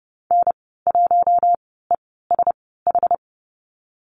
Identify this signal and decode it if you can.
Morse code, heard as a keyed tone, about 31 wpm, spaced as for 17 wpm, 700 Hz. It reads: N1ESH
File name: Lesson 14g - Number 2 - Call Signs 30wpm